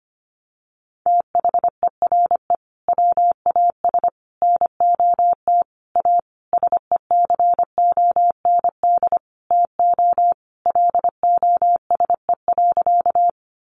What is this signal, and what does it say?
Morse code, heard as a keyed tone, about 25 wpm, 700 Hz. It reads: THERE WAS NOT A SECOND TO LOSE.
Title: Sentences from Top 1100 Words 25wpm